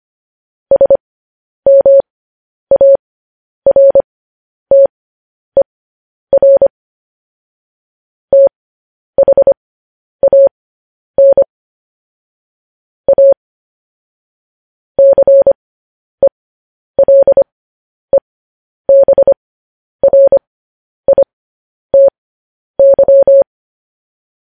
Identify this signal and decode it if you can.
Morse code, heard as a keyed tone, about 25 wpm, spaced as for 10 wpm, 550 Hz. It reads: SMARTER THAN A CELEBRITY
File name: podcast / News headlines-1x-25x10